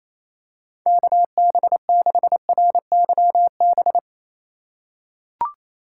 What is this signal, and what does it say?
Morse code, heard as a keyed tone, about 27 wpm, 700 Hz. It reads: KB6RYB
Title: USA Call Signs - 6 Characters - 4x Character Spacing - 28wpm